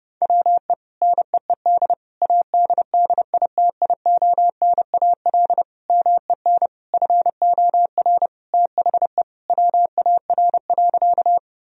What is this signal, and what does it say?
Morse code, heard as a keyed tone, about 30 wpm, 700 Hz. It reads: WE NEED ADDITIONAL MEN FOR THE WAR.
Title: Sentences from Top 1000 Words - Review Part 4 - 30wpm